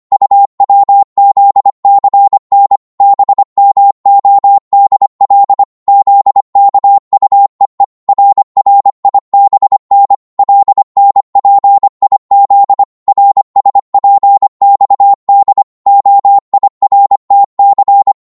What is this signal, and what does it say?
Morse code, heard as a keyed tone, about 25 wpm, 800 Hz. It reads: UWZCN BMODL ZKUEE RRIBN LNPIZ RSPXD OIRTC